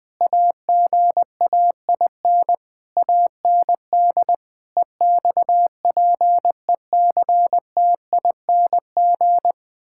Morse code, heard as a keyed tone, about 20 wpm, 700 Hz. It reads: AGAIN AND EXPECTING
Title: Sentences with 9-Word Phrases - Spoken Beginning - 20wpm